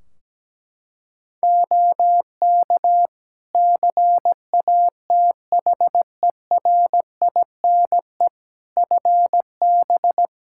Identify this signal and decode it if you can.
Morse code, heard as a keyed tone, about 17 wpm, 700 Hz. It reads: OK CATHERINE FB
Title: Practice QSO - Exchange 2 - Info - 17wpm